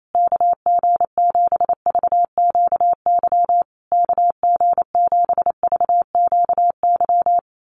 Morse code, heard as a keyed tone, about 28 wpm, 700 Hz. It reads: KG7VQY KG7VQY
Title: USA Call Signs Repeated 3x 28wpm